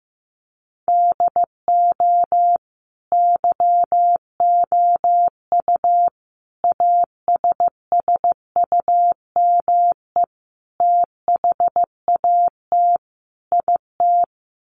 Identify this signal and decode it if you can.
Morse code, heard as a keyed tone, about 15 wpm, 700 Hz. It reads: DO YOU ASSUME THAT IT
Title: Sentences from Top 1500 Words 15wpm